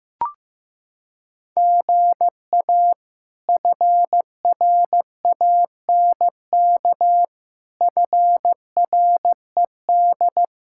Morse code, heard as a keyed tone, about 15 wpm, 700 Hz. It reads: GA FRANK FRED
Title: K1USN SST QSO Practice - 15wpm